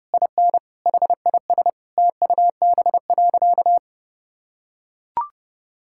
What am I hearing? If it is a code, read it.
Morse code, heard as a keyed tone, about 29 wpm, 700 Hz. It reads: IN HIS TUB.
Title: Pangram Challenge v2 - 30wpm